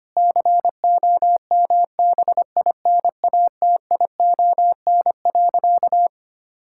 Morse code, heard as a keyed tone, about 25 wpm, 700 Hz. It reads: COMBINATION.
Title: Sentences from Top 1000 Words - Review Part 6 - 25wpm